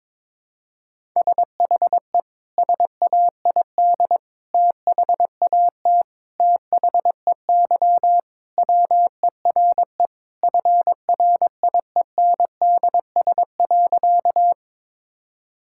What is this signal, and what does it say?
Morse code, heard as a keyed tone, about 22 wpm, 700 Hz. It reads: SHE SAID THAT THEY WERE FRIENDS.
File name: practice / 22wpm / Sentences with 3-Word Phrases - 22wpm